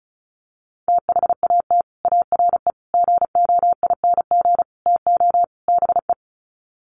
Morse code, heard as a keyed tone, about 34 wpm, 700 Hz. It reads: THAT ARE GOING TO BE
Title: Sets of 5 Words Encore2 35wpm